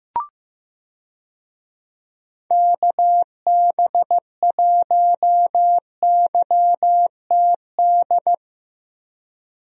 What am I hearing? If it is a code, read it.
Morse code, heard as a keyed tone, about 15 wpm, 700 Hz. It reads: KB1YTD
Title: USA Call Signs - 6 Characters - Repeated 2x - 15wpm